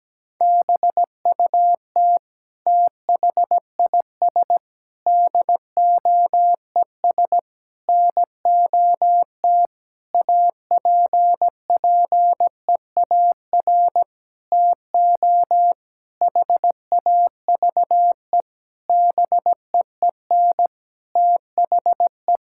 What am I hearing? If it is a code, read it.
Morse code, heard as a keyed tone, about 17 wpm, 700 Hz. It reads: BUT THIS DOES NOT APPEAR TO HAVE BEEN THE
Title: Sentences from Top 1000 Words - Review Part 6 - 17wpm